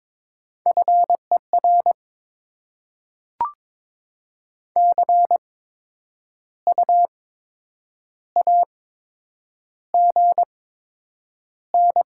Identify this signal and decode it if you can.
Morse code, heard as a keyed tone, about 22 wpm, spaced as for 8 wpm, 700 Hz. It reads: FER CU AGN
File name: Top 20 QSO Elements - 8x Character Spacing - 22wpm